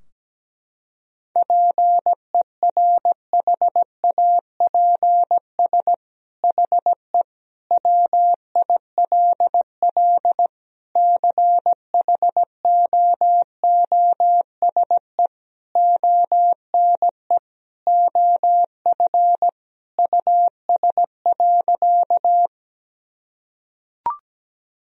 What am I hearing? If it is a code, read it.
Morse code, heard as a keyed tone, about 17 wpm, 700 Hz. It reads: PERHAPS HE WILL CHOOSE ONE OF US.
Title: Sentences from Top 1000 Words - Review Part 4 - 17wpm